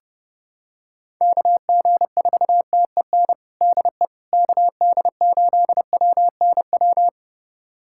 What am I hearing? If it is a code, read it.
Morse code, heard as a keyed tone, about 30 wpm, 700 Hz. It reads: KG4TEN DE KD8WNW
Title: USA Callsign DE Callsign - 6 Characters - 30wpm